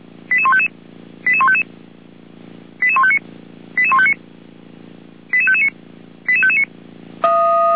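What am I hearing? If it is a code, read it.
Five-tone selective call (ZVEI), heard as quick runs of five tones, tones of about 70 ms. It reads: 88155 88155 88159 88159 88559 88559
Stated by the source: ZVEI five-tone selective call received by SDR.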